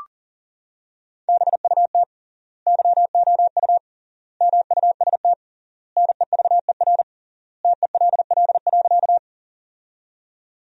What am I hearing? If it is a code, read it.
Morse code, heard as a keyed tone, about 37 wpm, 700 Hz. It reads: BUT YOU MUST NEVER TELL.
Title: 3x Word-Spacing - Sentences from Top 300 Words - 40wpm